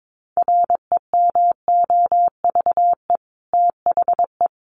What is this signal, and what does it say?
Morse code, heard as a keyed tone, about 22 wpm, 700 Hz. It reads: REMOVE THE
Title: Sentences from Top 1300 Words 22wpm